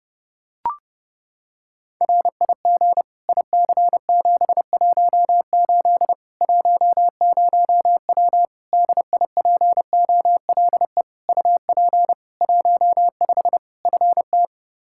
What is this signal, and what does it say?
Morse code, heard as a keyed tone, about 30 wpm, 700 Hz. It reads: RIG IC718 10W DIPOLE UP 15 FT
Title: Practice QSO - Exchange 2 - Info - 30wpm